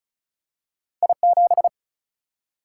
Morse code, heard as a keyed tone, about 36 wpm, 700 Hz. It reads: I7
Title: Single-Letter Single-Number 35wpm